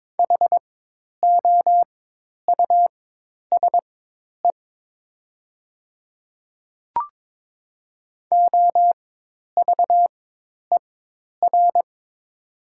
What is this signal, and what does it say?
Morse code, heard as a keyed tone, about 22 wpm, spaced as for 10 wpm, 700 Hz. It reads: HOUSE OVER
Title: 4x Character Spacing - Top 300 Words - 22wpm